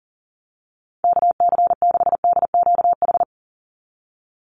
Morse code, heard as a keyed tone, about 40 wpm, 700 Hz. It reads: KC6DQH
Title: USA Call Signs 40wpm